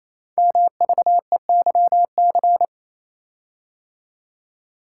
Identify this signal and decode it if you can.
Morse code, heard as a keyed tone, about 27 wpm, 690 Hz. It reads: MVEYC
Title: Code Groups - 5 Letters - 28wpm